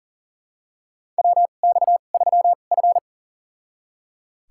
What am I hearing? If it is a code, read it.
Morse code, heard as a keyed tone, about 38 wpm, spaced as for 28 wpm, 690 Hz. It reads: WX3F
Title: USA Call Signs - 4 Characters - 2x Character Spacing - 40wpm